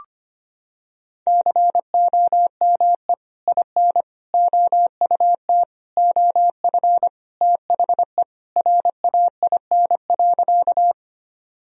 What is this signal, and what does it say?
Morse code, heard as a keyed tone, about 25 wpm, 700 Hz. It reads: COME IN OUT OF THE RAIN.
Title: Sentences from Top 1000 Words - Review Part 1 - 25wpm